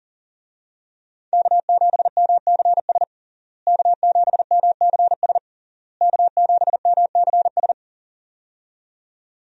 Morse code, heard as a keyed tone, about 37 wpm, 700 Hz. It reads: K7MCS K7MCS K7MCS
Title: USA Call Signs - 5 Characters - Repeated 3x - 40wpm